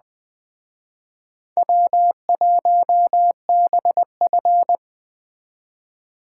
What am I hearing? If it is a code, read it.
Morse code, heard as a keyed tone, about 20 wpm, 700 Hz. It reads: W1BF